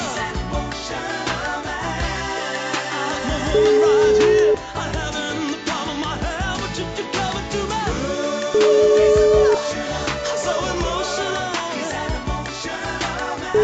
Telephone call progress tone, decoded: ringback tone

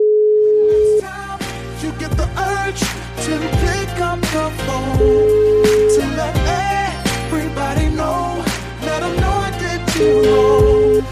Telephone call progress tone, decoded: ringback tone

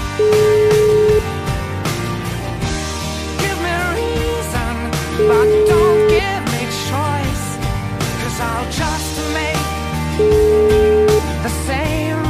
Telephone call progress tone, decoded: ringback tone

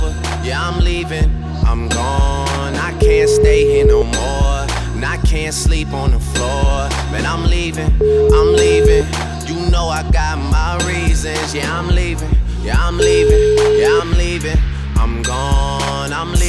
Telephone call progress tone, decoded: ringback tone

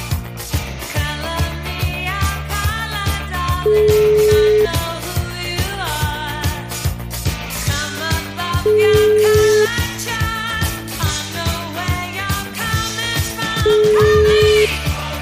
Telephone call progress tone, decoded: ringback tone

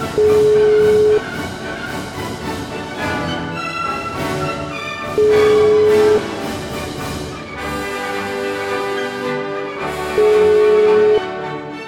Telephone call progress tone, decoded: ringback tone

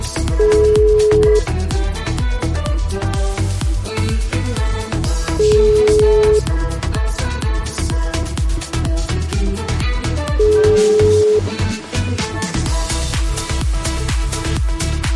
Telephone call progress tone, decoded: ringback tone

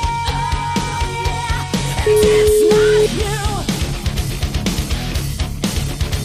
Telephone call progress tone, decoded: ringback tone

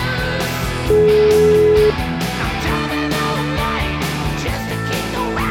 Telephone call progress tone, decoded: ringback tone

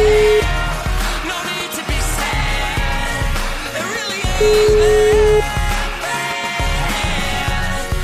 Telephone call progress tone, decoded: ringback tone

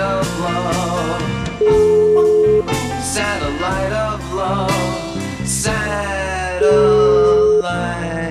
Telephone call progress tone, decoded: ringback tone

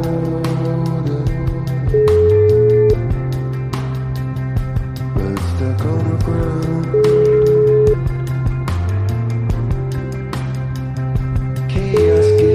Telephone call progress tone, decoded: ringback tone

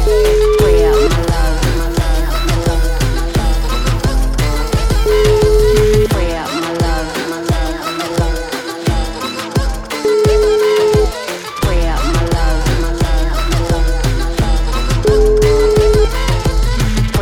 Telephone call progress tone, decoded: ringback tone

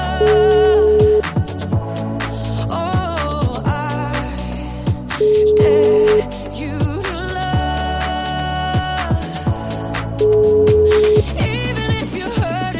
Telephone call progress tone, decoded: ringback tone

